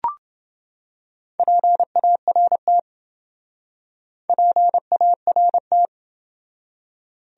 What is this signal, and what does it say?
Morse code, heard as a keyed tone, about 28 wpm, 700 Hz. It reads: PART PART